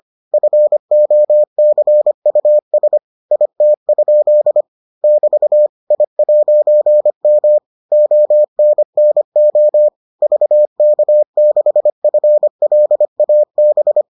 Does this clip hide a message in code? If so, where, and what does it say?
Yes: Morse code from 0.3 to 14.0 s, heard as a keyed tone, about 25 wpm, 600 Hz. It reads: FOCUS IT? <BT> I'M ONNO VK6FLAB